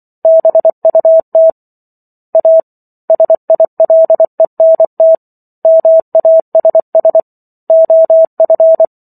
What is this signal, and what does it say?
Morse code, heard as a keyed tone, about 24 wpm, 650 Hz. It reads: BUT A SILENT MASS OF
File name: Chapter 15 - 24 WPM